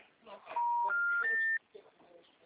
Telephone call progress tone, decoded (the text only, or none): special information tone